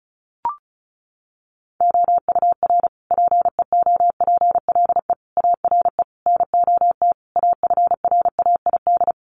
Morse code, heard as a keyed tone, about 34 wpm, 700 Hz. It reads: OUR PEOPLE ARE NOT AFRAID
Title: Sentences from Top 1500 Words 35wpm